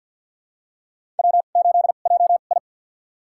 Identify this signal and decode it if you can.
Morse code, heard as a keyed tone, about 53 wpm, spaced as for 37 wpm, 700 Hz. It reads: W8JI
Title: USA Call Signs - 4 Characters - 2x Character Spacing - 50wpm